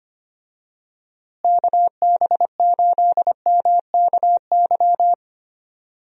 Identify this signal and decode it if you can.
Morse code, heard as a keyed tone, about 25 wpm, 700 Hz. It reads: KB8MKY
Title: USA Call Signs - 6 Characters - Repeated 3x - 25wpm